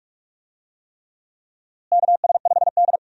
Morse code, heard as a keyed tone, about 46 wpm, 700 Hz. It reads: KS5D
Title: USA Call Signs - 4 Characters - Repeated 3x - 45wpm